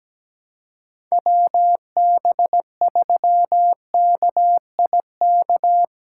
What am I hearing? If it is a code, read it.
Morse code, heard as a keyed tone, about 17 wpm, 700 Hz. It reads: WB3KIK
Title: USA Call Signs - 4x Character Spacing - 17wpm